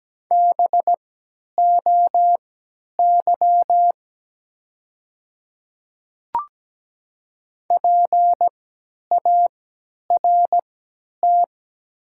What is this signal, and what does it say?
Morse code, heard as a keyed tone, about 17 wpm, spaced as for 10 wpm, 700 Hz. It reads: BOY PART
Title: 3x Character Spacing - Top 500 Words - 17wpm